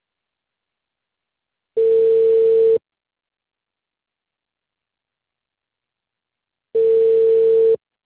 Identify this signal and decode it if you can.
Ringback tone (China, 450 Hz).